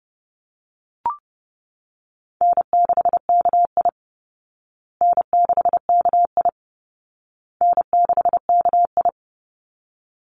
Morse code, heard as a keyed tone, about 30 wpm, 700 Hz. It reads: N6KI N6KI N6KI
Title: Easy Call Signs Repeated 3x - 30wpm